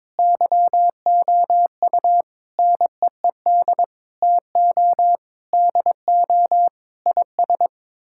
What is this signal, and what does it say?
Morse code, heard as a keyed tone, about 21 wpm, 700 Hz. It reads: YOU NEED TO DO IS
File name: practice / 22wpm / Sets of 8 Words - 22wpm